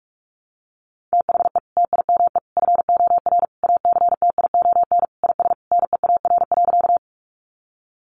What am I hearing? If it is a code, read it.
Morse code, heard as a keyed tone, about 44 wpm, 700 Hz. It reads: THE TIME FOR ACTION IS NEAR.
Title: Sentences from Top 700 Words 45wpm